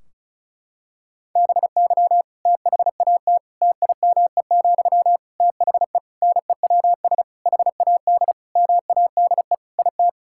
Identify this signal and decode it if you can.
Morse code, heard as a keyed tone, about 34 wpm, 700 Hz. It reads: BY THAT TIME, THE NEWS HAD MADE IT
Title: Sentences from Top 1600 Words 35wpm